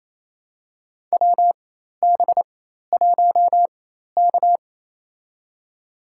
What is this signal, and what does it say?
Morse code, heard as a keyed tone, about 28 wpm, spaced as for 13 wpm, 700 Hz. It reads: WB1K
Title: USA Call Signs - 4x Character Spacing - 28wpm